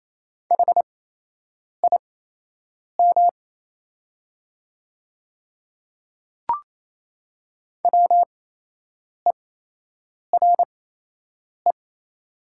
Morse code, heard as a keyed tone, about 28 wpm, spaced as for 8 wpm, 700 Hz. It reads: HIM WERE
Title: 8x Character Spacing - Top 100 Words - 28wpm